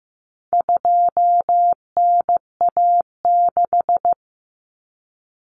Morse code, heard as a keyed tone, about 15 wpm, 700 Hz. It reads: 2NA6
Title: Common Cut Numbers - 4-Digit 15wpm